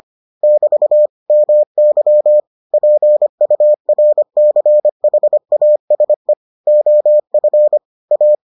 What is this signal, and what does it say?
Morse code, heard as a keyed tone, about 25 wpm, 600 Hz. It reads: <BT> MY PURCHASE OF A